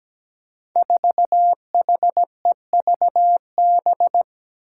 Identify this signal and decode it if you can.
Morse code, heard as a keyed tone, about 17 wpm, 700 Hz. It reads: 4HEVB
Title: Five Challenging Letter-Number Combinations 17wpm